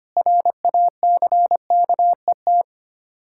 Morse code, heard as a keyed tone, about 25 wpm, 700 Hz. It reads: RACKET
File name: practice / 25wpm / Difficult Words with Hints - 25wpm